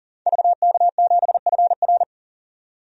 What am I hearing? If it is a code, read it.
Morse code, heard as a keyed tone, about 40 wpm, 700 Hz. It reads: VK7FR
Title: DXCC-100 Call Signs - 5 Characters - 40wpm